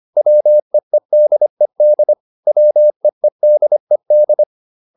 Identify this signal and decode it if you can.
Morse code, heard as a keyed tone, about 25 wpm, 600 Hz. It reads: WEEDED WEEDED